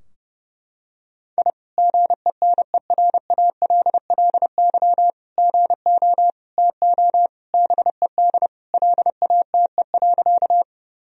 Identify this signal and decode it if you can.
Morse code, heard as a keyed tone, about 31 wpm, 700 Hz. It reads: I GENERALLY GO TO BED LATE.
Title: Sentences from Top 1000 Words - Review Part 4 - 30wpm